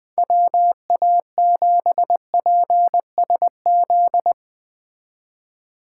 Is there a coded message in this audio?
Yes: Morse code from 0.2 to 4.3 s, heard as a keyed tone, about 21 wpm, 700 Hz. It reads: WA7PSZ